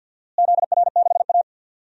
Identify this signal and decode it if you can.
Morse code, heard as a keyed tone, about 48 wpm, 700 Hz. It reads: ZR6A
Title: DXCC-100 Call Signs - 4 Characters - Repeated 2x - 50wpm